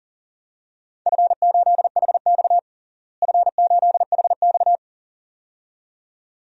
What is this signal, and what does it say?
Morse code, heard as a keyed tone, about 41 wpm, 700 Hz. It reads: F8HX F8HX
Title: DXCC-100 Call Signs - 4 Characters - Repeated 2x - 40wpm